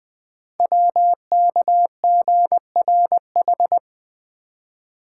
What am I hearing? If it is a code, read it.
Morse code, heard as a keyed tone, about 20 wpm, 700 Hz. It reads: WKGRH
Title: Code Groups - 5 Letters - 20wpm